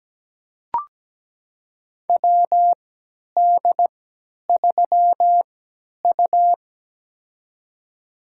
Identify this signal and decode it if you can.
Morse code, heard as a keyed tone, about 17 wpm, spaced as for 10 wpm, 700 Hz. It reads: WD3U
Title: USA Call Signs - 3x Character Spacing - 17wpm